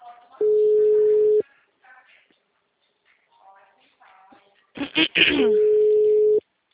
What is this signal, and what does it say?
Ringback tone (Europe, 425 Hz), with speech.
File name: out- -2012- - - .65875